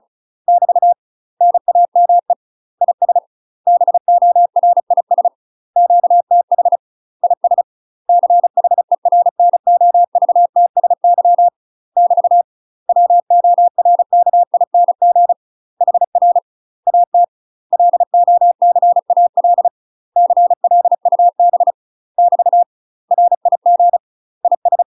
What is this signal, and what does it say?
Morse code, heard as a keyed tone, about 34 wpm, 700 Hz. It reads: <BT> NAME IS BORIS QTH IS CHERNOVTSY <BT> WORKING HR AT LOCAL CLUB <BT> RIG IS
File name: UK MASTER QSOs TRAINING / 35WPM / UK MASTER QSOs 0004